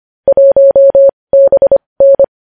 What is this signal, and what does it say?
Morse code, heard as a keyed tone, about 25 wpm, spaced as for 20 wpm, 550 Hz. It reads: 1BN
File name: podcast / News headlines-1x-25x20